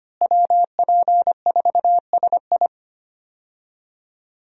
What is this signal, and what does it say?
Morse code, heard as a keyed tone, about 25 wpm, 700 Hz. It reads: WP4SI
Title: USA Call Signs - 5 Characters - Repeated 3x - 25wpm